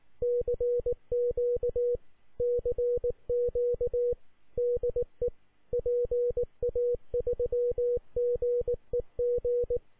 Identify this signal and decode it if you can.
Morse code, heard as a keyed tone, about 19 wpm, 480 Hz. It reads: CQ CQ DE PA3GEG